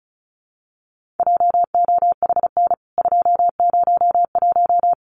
Morse code, heard as a keyed tone, about 34 wpm, 700 Hz. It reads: JOHN 201
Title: CWT Exchange 35wpm